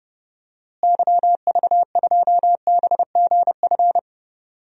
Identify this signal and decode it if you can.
Morse code, heard as a keyed tone, about 30 wpm, 700 Hz. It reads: YV2BGF
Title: DXCC-100 Call Signs - 6 Characters - Repeated 2x - 30wpm